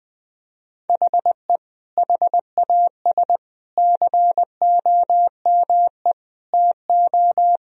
Morse code, heard as a keyed tone, about 20 wpm, 700 Hz. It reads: HE HAS COME TO
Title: Sentences from Top 1000 Words - Review Part 1 - 20wpm